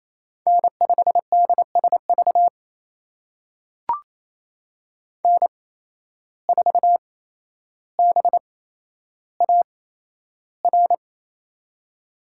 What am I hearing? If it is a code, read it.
Morse code, heard as a keyed tone, about 28 wpm, spaced as for 12 wpm, 700 Hz. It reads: N5DSV N4BAR
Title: USA Call Signs - 5 Characters - 8x Character Spacing - 28wpm